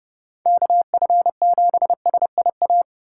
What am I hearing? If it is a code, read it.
Morse code, heard as a keyed tone, about 30 wpm, 700 Hz. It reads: KF7SIA